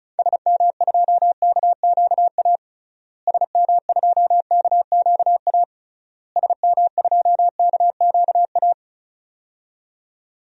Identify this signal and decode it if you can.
Morse code, heard as a keyed tone, about 34 wpm, 700 Hz. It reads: SM2KQA SM2KQA SM2KQA
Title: DXCC-100 Call Signs - 6 Characters - Repeated 3x - 35wpm